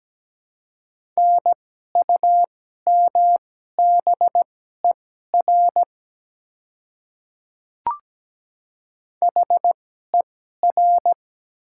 Morse code, heard as a keyed tone, about 17 wpm, spaced as for 12 wpm, 700 Hz. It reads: NUMBER HER
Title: 2x Character Spacing - Top 100 Words - 17wpm